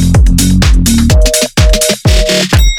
Reorder tone (North America), 480 + 620 Hz.